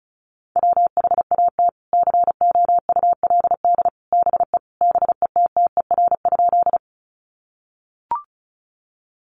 Morse code, heard as a keyed tone, about 34 wpm, 700 Hz. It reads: WHAT COULD BE BETTER?
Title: Sentences from Top 400 Words 35wpm